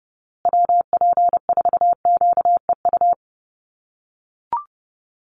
Morse code, heard as a keyed tone, about 30 wpm, 700 Hz. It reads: WP4QEU
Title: USA Call Signs Repeated 3x 30wpm